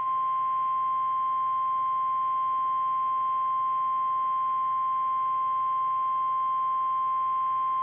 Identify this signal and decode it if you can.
Emergency Alert System (SAME) tones (the 1050 Hz attention tone). It is heard from 1.0 to 7.3 s.
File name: eas / KKPRKXPN / KKPRKXPN